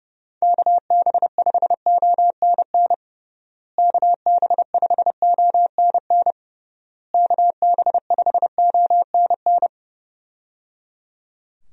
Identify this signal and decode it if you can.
Morse code, heard as a keyed tone, about 29 wpm, 700 Hz. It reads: KB5ONN KB5ONN KB5ONN
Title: USA Call Signs - 6 Characters - Repeated 3x - 30wpm